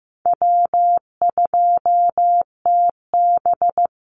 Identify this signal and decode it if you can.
Morse code, heard as a keyed tone, about 15 wpm, 700 Hz. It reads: W2TB
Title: Easy Call Signs Repeated 3x - 15wpm